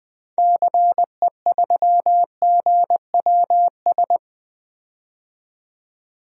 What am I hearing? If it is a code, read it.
Morse code, heard as a keyed tone, about 20 wpm, 700 Hz. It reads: CE3GWS
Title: DXCC-100 Call Signs - 6 Characters - 20wpm